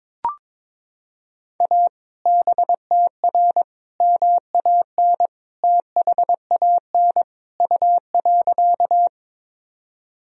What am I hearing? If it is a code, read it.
Morse code, heard as a keyed tone, about 22 wpm, 700 Hz. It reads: A BTR MAN THAN U.
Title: Sentences from Top 400 Words and Abbreviations 22wpm